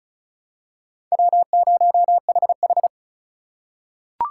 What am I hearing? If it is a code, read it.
Morse code, heard as a keyed tone, about 36 wpm, 700 Hz. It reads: W0HH
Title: USA Call Signs - 4 Characters - 4x Character Spacing - 35wpm